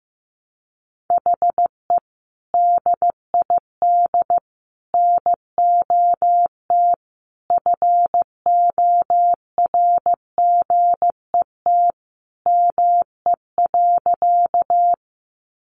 Morse code, heard as a keyed tone, about 15 wpm, 700 Hz. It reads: HE DID NOT FORGET ME.
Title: Sentences from Top 1200 Words 15wpm